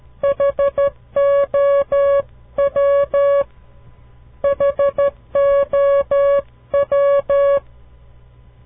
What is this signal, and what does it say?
Morse code, heard as a keyed tone, about 13 wpm, 560 Hz. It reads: HOW HOW
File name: CW Learning Programs / farnsworth / unit 2